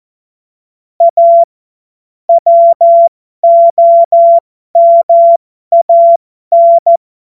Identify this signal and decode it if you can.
Morse code, heard as a keyed tone, about 14 wpm, spaced as for 12 wpm, 680 Hz. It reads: A WOMAN
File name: Morsekurs / Witze